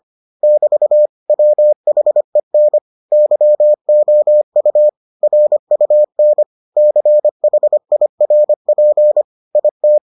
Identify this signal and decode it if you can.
Morse code, heard as a keyed tone, about 25 wpm, 600 Hz. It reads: <BT> WHEN YOU RUN CHIRP IT